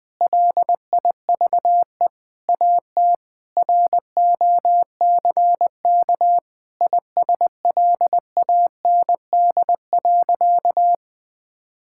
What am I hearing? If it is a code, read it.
Morse code, heard as a keyed tone, about 20 wpm, 700 Hz. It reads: LIVE AT ROCK ISLAND.
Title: Sentences from Top 1600 Words 20wpm